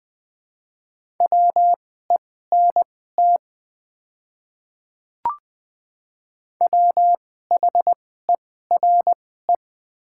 Morse code, heard as a keyed tone, about 20 wpm, spaced as for 14 wpm, 700 Hz. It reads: WENT WHERE
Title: 2x Character Spacing - Top 300 Words - 20wpm